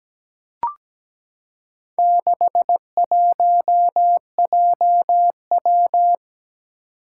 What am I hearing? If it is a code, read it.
Morse code, heard as a keyed tone, about 17 wpm, 700 Hz. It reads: 61JW
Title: Four Challenging Letter-Number Combinations 17wpm